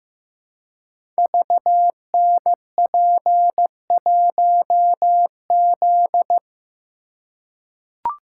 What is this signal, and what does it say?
Morse code, heard as a keyed tone, about 15 wpm, 700 Hz. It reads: VNP1Z
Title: Code Groups - 5 Letter:Numbers - 15wpm